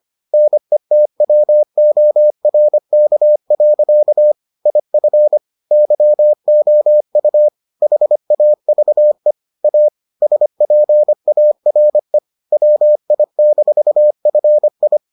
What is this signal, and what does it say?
Morse code, heard as a keyed tone, about 25 wpm, 600 Hz. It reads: NETWORK. IF YOU HAVE A SPARE WI-FI